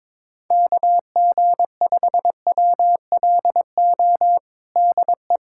Morse code, heard as a keyed tone, about 21 wpm, 700 Hz. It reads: KG5WLO DE
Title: USA Callsign DE Callsign 22wpm